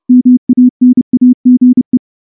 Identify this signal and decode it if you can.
Morse code, heard as a keyed tone, about 30 wpm, 250 Hz. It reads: MANAGE
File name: morse / morse-training-30WPM